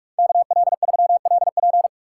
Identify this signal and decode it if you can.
Morse code, heard as a keyed tone, about 46 wpm, 700 Hz. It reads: KL3LP